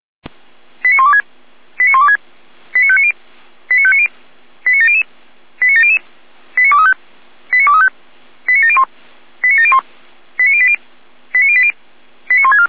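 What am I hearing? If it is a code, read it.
Five-tone selective call (ZVEI), heard as quick runs of five tones, tones of about 70 ms. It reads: 89126 89126 89590 89590 89700 89700 89235 89235 89701 89701 89080 89080 89164
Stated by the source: ZVEI five-tone selective call 89340 received on SDR.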